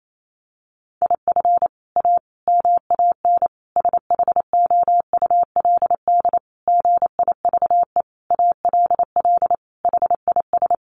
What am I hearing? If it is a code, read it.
Morse code, heard as a keyed tone, about 28 wpm, 700 Hz. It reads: IF A MAN SHOULD GIVE ALL HIS
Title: Sentences from Top 1500 Words 28wpm